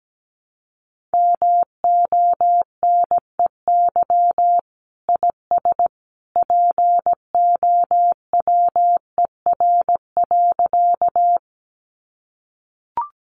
Morse code, heard as a keyed tone, about 17 wpm, 700 Hz. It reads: MONEY IS POWER.
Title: Sentences from Top 500 Words 17wpm